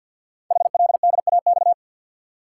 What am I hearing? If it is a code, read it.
Morse code, heard as a keyed tone, about 50 wpm, 690 Hz. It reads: HLDAX
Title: Code Groups - 5 Letters - 50wpm